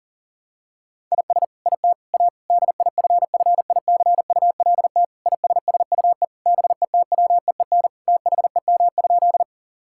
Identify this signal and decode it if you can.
Morse code, heard as a keyed tone, about 40 wpm, 700 Hz. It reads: IS IT A DIFFICULT ISSUE BETWEEN THEM?
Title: Sentences from Top 1000 Words - Review Part 6 - 40wpm